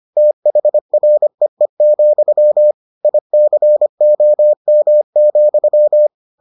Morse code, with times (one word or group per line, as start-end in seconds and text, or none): 0.2-2.7 THREE,
3.0-6.1 ICOM,